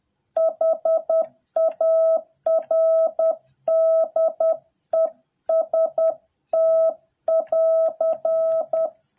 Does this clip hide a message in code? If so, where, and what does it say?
Yes: Morse code from 0.4 to 8.9 s, heard as a keyed tone, about 10 wpm, 650 Hz. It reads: HARDEST<AR>